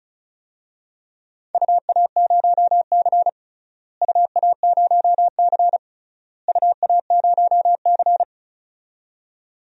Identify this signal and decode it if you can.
Morse code, heard as a keyed tone, about 36 wpm, 700 Hz. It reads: UA0C UA0C UA0C
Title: DXCC-100 Call Signs - 4 Characters - Repeated 3x - 35wpm